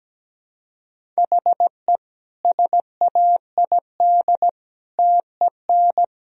Morse code, heard as a keyed tone, about 17 wpm, 700 Hz. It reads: HE SAID TEN
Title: Sentences from Top 1000 Words - Review Part 3 - 17wpm